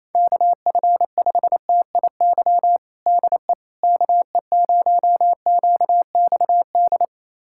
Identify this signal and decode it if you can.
Morse code, heard as a keyed tone, about 27 wpm, 700 Hz. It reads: KF5TIY DE KE0QXD
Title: USA Callsign DE Callsign - 6 Characters - 28wpm